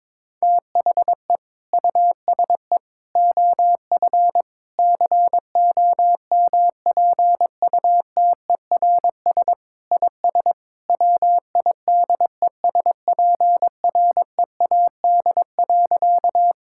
Morse code, heard as a keyed tone, about 22 wpm, 700 Hz. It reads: THE USE OF COMPUTERS IS WIDESPREAD.
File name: practice / 22wpm / Sentences with 3-Word Phrases - 22wpm